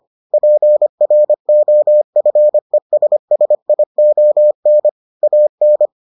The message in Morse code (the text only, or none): PROFESSION AN